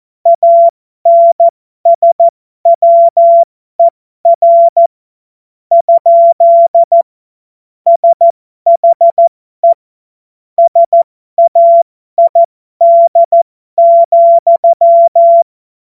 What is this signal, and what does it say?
Morse code, heard as a keyed tone, about 14 wpm, spaced as for 11 wpm, 680 Hz. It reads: ANSWER? SHE SAID,
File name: Morsekurs / Witze